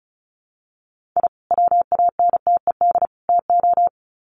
Morse code, heard as a keyed tone, about 34 wpm, 700 Hz. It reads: I WANTED TO
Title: Sets of 3 Words 35wpm